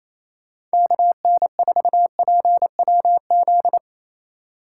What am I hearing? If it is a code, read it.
Morse code, heard as a keyed tone, about 34 wpm, 700 Hz. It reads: KN4PWZ
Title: Speed Racing - USA Call Signs 28wpm